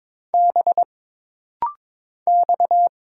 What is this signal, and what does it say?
Morse code, heard as a keyed tone, about 22 wpm, 700 Hz. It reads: B X
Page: This rapid-fire timing is geared to push your ICR (Instant Character Recognition) skills.